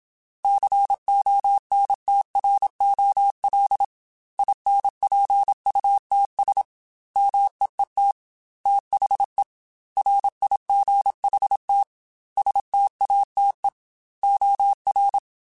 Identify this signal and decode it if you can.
Morse code, heard as a keyed tone, about 27 wpm, 800 Hz. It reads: CONTROL INPUTS MEET THE RIGHT STATE OR
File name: Technical Subjects / Technical Subjects-mp3s / OR Gate @25